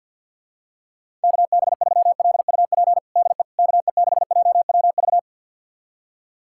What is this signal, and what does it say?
Morse code, heard as a keyed tone, about 47 wpm, 700 Hz. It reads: KB3LUP DE KE6JWV
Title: USA Callsign DE Callsign 50wpm